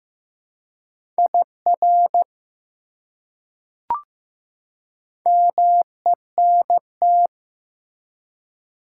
Morse code, heard as a keyed tone, about 15 wpm, 700 Hz. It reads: IR MENT
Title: Prefixes and Suffixes 15wpm